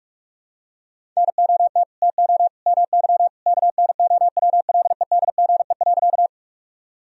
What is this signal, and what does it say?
Morse code, heard as a keyed tone, about 41 wpm, 700 Hz. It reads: NOT TO MY KNOWLEDGE.
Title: Sentences from Top 1000 Words - Review Part 3 - 45wpm